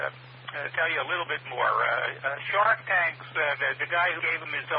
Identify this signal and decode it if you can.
Emergency Alert System (SAME) tones, heard as harsh buzzing bursts, with speech. It is heard from 0.5 to 3.7 s.